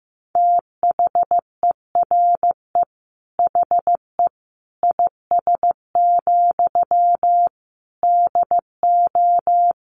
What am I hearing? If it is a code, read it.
Morse code, heard as a keyed tone, about 15 wpm, 700 Hz. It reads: THERE HE IS, DO
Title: Sentences from Top 100 Words 15wpm